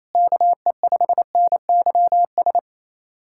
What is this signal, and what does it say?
Morse code, heard as a keyed tone, about 28 wpm, 700 Hz. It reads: KE5NYS
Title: USA Callsign DE Callsign 28wpm